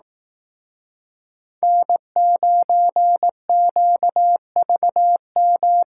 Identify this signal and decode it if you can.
Morse code, heard as a keyed tone, about 18 wpm, 700 Hz. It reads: N9QVM
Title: Speed Racing - USA Call Signs 15wpm